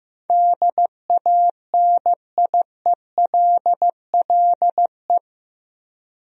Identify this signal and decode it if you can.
Morse code, heard as a keyed tone, about 15 wpm, 700 Hz. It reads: DANIELLE
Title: USA Common Names - 15wpm